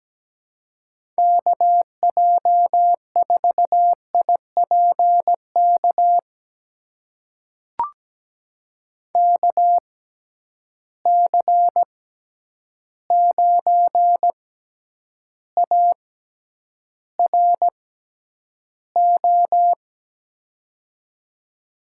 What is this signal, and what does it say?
Morse code, heard as a keyed tone, about 17 wpm, spaced as for 9 wpm, 700 Hz. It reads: KJ4IPK KC9ARO
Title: USA Call Signs - 6 Characters - 6x Character Spacing - 17wpm